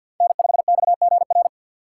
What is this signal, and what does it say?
Morse code, heard as a keyed tone, about 47 wpm, 700 Hz. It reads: N5XGR
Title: USA Call Signs - 5 Characters - 6x Character Spacing - 50wpm